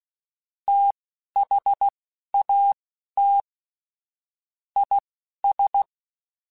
Morse code, heard as a keyed tone, about 16 wpm, spaced as for 12 wpm, 800 Hz. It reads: THAT IS
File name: Technical Subjects / Technical Subjects-mp3s / XL @10